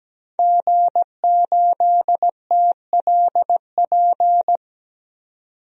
Morse code, heard as a keyed tone, about 17 wpm, 700 Hz. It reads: G8TLP
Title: DXCC-100 Call Signs - 5 Characters - 17wpm